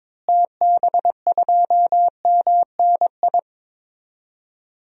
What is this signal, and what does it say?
Morse code, heard as a keyed tone, about 22 wpm, 700 Hz. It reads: TB2MNI
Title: DXCC-100 Call Signs - 6 Characters - Repeated 2x - 22wpm